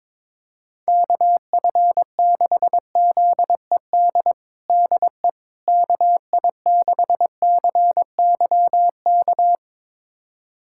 Morse code, heard as a keyed tone, about 22 wpm, 700 Hz. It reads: KF6ZED DE KI6CYK
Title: USA Callsign DE Callsign - 6 Characters - 22wpm